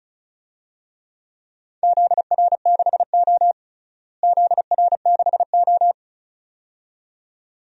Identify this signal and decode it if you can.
Morse code, heard as a keyed tone, about 36 wpm, 700 Hz. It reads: ZR6O ZR6O
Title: DXCC-100 Call Signs - 4 Characters - Repeated 2x - 35wpm